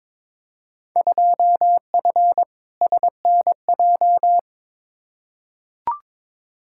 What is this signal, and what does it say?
Morse code, heard as a keyed tone, about 22 wpm, 700 Hz. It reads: 2F SNJ
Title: Field Day Exchanges Encore - 22wpm